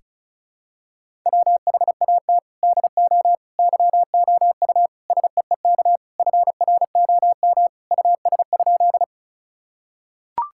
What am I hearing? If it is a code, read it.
Morse code, heard as a keyed tone, about 34 wpm, 700 Hz. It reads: WHAT DO YOU SEEK FROM US?
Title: Sentences from Top 1600 Words 35wpm